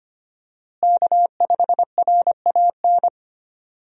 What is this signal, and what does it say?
Morse code, heard as a keyed tone, about 25 wpm, 700 Hz. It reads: K5RAN
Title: USA Call Signs - 5 Characters - Repeated 3x - 25wpm